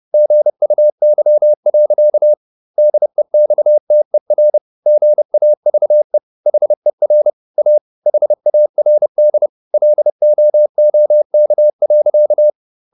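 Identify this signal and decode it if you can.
Morse code, heard as a keyed tone, about 30 wpm, 600 Hz. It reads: GUY. DEXTER GAVE HER A HARD LOOK.
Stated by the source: Morse code at 30 wpm, the novel Never by Ken Follett in English, from CW med Gnister.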